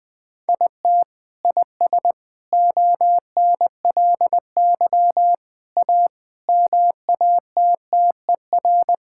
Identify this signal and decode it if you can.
Morse code, heard as a keyed tone, about 20 wpm, 700 Hz. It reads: IT IS ONLY A MATTER
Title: Sentences with 6-Word Phrases - 20wpm